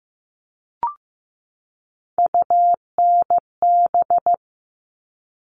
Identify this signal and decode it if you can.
Morse code, heard as a keyed tone, about 15 wpm, 700 Hz. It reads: UNB
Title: All Cut Numbers - 3-Digit 15wpm